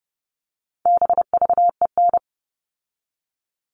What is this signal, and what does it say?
Morse code, heard as a keyed tone, about 30 wpm, 700 Hz. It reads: BVEN
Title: All Cut Numbers - 4-Digit 30wpm